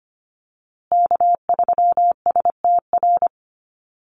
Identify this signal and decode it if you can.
Morse code, heard as a keyed tone, about 25 wpm, 700 Hz. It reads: K3STR
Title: USA Call Signs Repeated 2x 25wpm